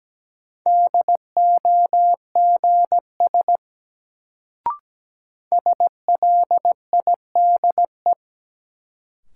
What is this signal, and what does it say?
Morse code, heard as a keyed tone, about 17 wpm, 700 Hz. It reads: DOGS SLIDE
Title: Sentences from Top 1600 Words - New Word Drill Alt - 17wpm